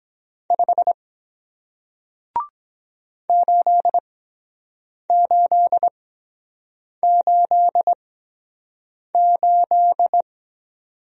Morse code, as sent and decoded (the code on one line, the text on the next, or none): ..... / ---.. ---.. ---.. ---..
5 8888